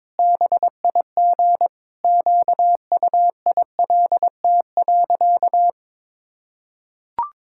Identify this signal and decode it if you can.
Morse code, heard as a keyed tone, about 22 wpm, 700 Hz. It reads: BIG QUILT.
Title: Pangram Challenge v2 - 22wpm